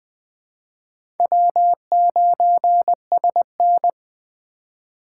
Morse code, heard as a keyed tone, about 20 wpm, 700 Hz. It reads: W9SN
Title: USA Call Signs - 4 Characters - 2x Character Spacing - 20wpm